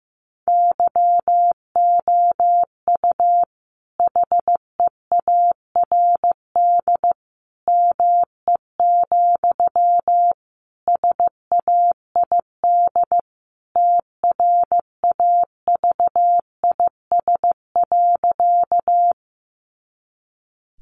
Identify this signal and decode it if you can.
Morse code, heard as a keyed tone, about 15 wpm, 700 Hz. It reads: YOU HEARD ME, SAID TRAVIS.
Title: A Sound Of Thunder 15wpm